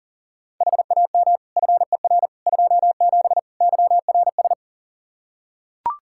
Morse code, heard as a keyed tone, about 40 wpm, 700 Hz. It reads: HAM FER 27 YRS
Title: Practice QSO - Exchange 3 - More Info - 40wpm